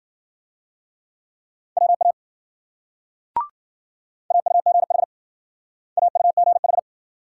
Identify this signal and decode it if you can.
Morse code, heard as a keyed tone, about 50 wpm, 700 Hz. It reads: WA AUGH AUGH